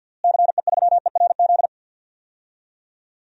Morse code, heard as a keyed tone, about 52 wpm, 700 Hz. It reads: CE3ERZ
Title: DXCC-100 Call Signs - 6 Characters - Repeated 3x - 50wpm